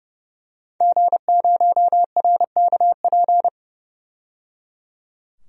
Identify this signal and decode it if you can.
Morse code, heard as a keyed tone, about 30 wpm, 700 Hz. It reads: G0RKP
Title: DXCC-100 Call Signs - 5 Characters - 30wpm